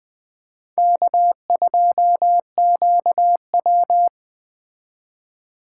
Morse code, heard as a keyed tone, about 21 wpm, 700 Hz. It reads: K2QW
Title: Speed Racing - USA 1x2 Call Signs 17wpm